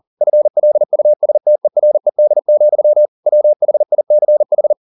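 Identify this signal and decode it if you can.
Morse code, heard as a keyed tone, about 40 wpm, 600 Hz. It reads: FLUSTERED, WHICH
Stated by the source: Morse code at 40 wpm, practice text, CW med Gnister.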